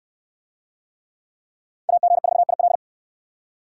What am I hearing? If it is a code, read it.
Morse code, heard as a keyed tone, about 65 wpm, 700 Hz. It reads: AG4IL